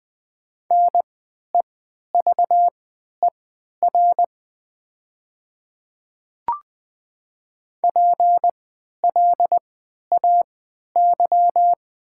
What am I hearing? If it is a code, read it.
Morse code, heard as a keyed tone, about 20 wpm, spaced as for 11 wpm, 700 Hz. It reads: NEVER PLAY
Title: 3x Character Spacing - Top 300 Words - 20wpm